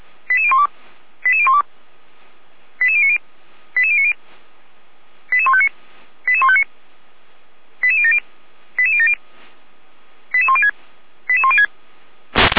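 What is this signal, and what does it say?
Five-tone selective call (ZVEI), heard as quick runs of five tones, tones of about 70 ms. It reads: 88012 88012 88090 88090 88159 88159 88070 88070 88106 88106
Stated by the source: ZVEI five-tone selective call 88102 received on an SDR, fire brigade (Feuerwehr) Orlamünde.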